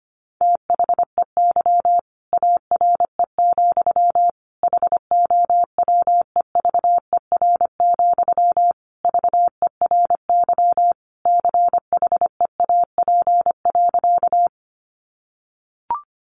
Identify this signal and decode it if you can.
Morse code, heard as a keyed tone, about 25 wpm, 700 Hz. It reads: THEY ARE, HOWEVER, VERY CHEAP.
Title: Sentences from Top 1200 Words 25wpm